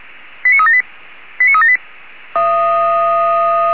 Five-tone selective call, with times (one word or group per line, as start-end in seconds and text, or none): 0.4-0.8 89378
1.4-1.8 89378